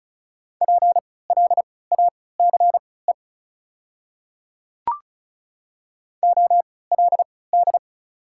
Morse code, heard as a keyed tone, about 34 wpm, spaced as for 20 wpm, 690 Hz. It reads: PLACE OLD
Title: 3x Character Spacing - Top 200 Words - 35wpm